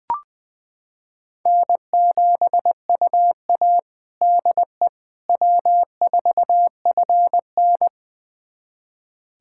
Morse code, heard as a keyed tone, about 20 wpm, 700 Hz. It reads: N7UA DE W4FN
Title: USA Callsign DE Callsign - 4 Characters - 20wpm